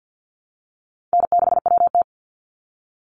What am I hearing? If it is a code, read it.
Morse code, heard as a keyed tone, about 50 wpm, 700 Hz. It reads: N6WT
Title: NA SOTA Chaser Callsigns 50wpm